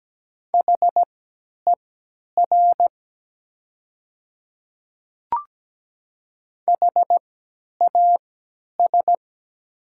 Morse code, heard as a keyed tone, about 17 wpm, spaced as for 10 wpm, 700 Hz. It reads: HER HAS
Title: 3x Character Spacing - Top 100 Words - 17wpm